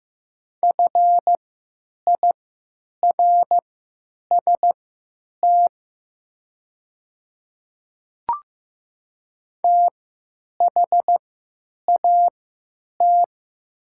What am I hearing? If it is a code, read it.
Morse code, heard as a keyed tone, about 15 wpm, spaced as for 9 wpm, 700 Hz. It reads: FIRST THAT
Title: 3x Character Spacing - Top 200 Words - 15wpm